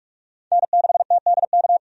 Morse code, heard as a keyed tone, about 46 wpm, 700 Hz. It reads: N6TDK
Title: USA Call Signs - 5 Characters - 45wpm